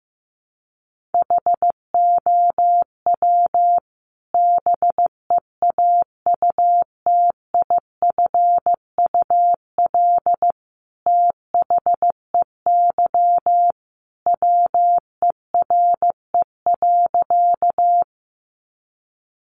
Morse code, heard as a keyed tone, about 15 wpm, 700 Hz. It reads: HOW BEAUTIFUL THEY WERE.
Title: Sentences from Top 900 Words 15wpm